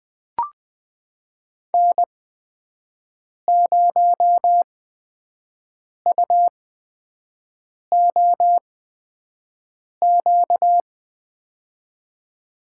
Morse code, heard as a keyed tone, about 20 wpm, spaced as for 5 wpm, 700 Hz. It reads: N0UOQ
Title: USA Call Signs - 5 Characters - 8x Character Spacing - 20wpm